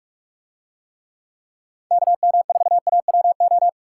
Morse code, heard as a keyed tone, about 45 wpm, 700 Hz. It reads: KM4AWO